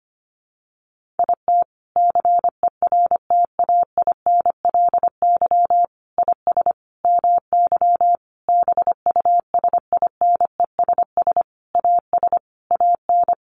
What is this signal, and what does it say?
Morse code, heard as a keyed tone, about 25 wpm, 700 Hz. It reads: IT CERTAINLY IS MY BUSINESS AS AN
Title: Sentences from Top 1200 Words 25wpm